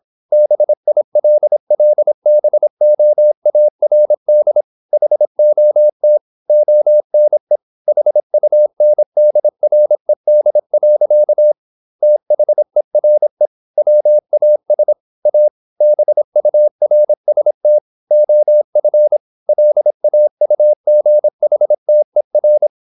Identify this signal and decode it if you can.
Morse code, heard as a keyed tone, about 26 wpm, 600 Hz. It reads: BILLBOARD HOT ONE HUNDRED. THERE WAS A BURST OF LAUGHTER
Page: Never 26wpm | CW med Gnister